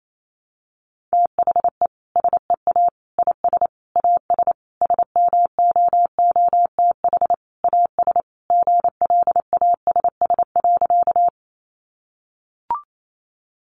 Morse code, heard as a keyed tone, about 28 wpm, 700 Hz. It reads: THE SEA IS AS SMOOTH AS GLASS.
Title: Sentences from Top 1400 Words 28wpm